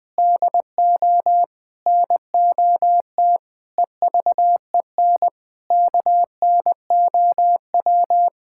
Morse code, heard as a keyed tone, about 21 wpm, 700 Hz. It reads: DO NOT EVEN KNOW
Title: Sentences with 4-Word Phrases - Spoken Beginning - 20wpm